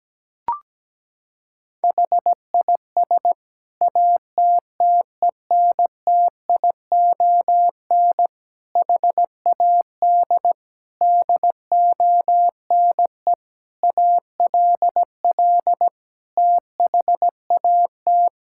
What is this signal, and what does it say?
Morse code, heard as a keyed tone, about 17 wpm, 700 Hz. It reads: HIS ATTENTION HAD DONE ALL THAT
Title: Sentences from Top 1000 Words - Review Part 6 - 17wpm